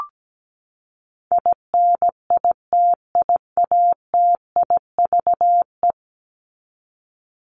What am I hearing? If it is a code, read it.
Morse code, heard as a keyed tone, about 17 wpm, 700 Hz. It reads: INITIATIVE
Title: Top 1751-2000 Words 17wpm